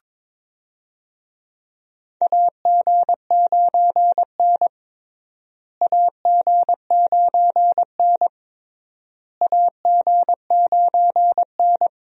Morse code, heard as a keyed tone, about 22 wpm, 700 Hz. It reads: AG9N AG9N AG9N
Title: USA Call Signs - 4 Characters - Repeated 3x - 22wpm